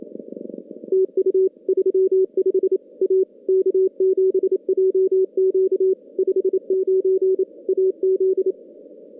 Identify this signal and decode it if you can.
Morse code, heard as a keyed tone, about 27 wpm, 380 Hz. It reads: TU 35 A K7JQ 59 AZ
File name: contests / CW-SS-k5tr- -2200